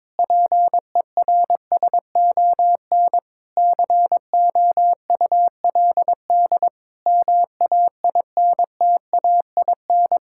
Morse code, heard as a keyed tone, about 22 wpm, 700 Hz. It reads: PERSON COULD MAINTAIN
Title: Sentences from Top 1000 Words - Review Part 5 - 22wpm